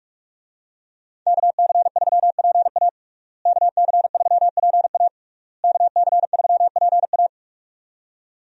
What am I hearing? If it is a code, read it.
Morse code, heard as a keyed tone, about 46 wpm, 700 Hz. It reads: KC3PA KC3PA KC3PA
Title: USA Call Signs - 5 Characters - Repeated 3x - 45wpm